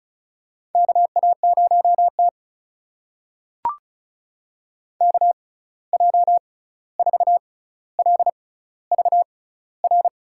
Morse code, heard as a keyed tone, about 36 wpm, spaced as for 16 wpm, 700 Hz. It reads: KA0T KJ4LVR
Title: USA Call Signs - 6x Character Spacing - 35wpm